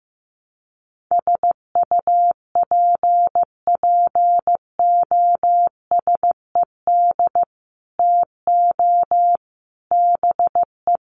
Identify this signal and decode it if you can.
Morse code, heard as a keyed tone, about 15 wpm, 700 Hz. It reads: SUPPOSED TO BE
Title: Sets of 3 Words 15wpm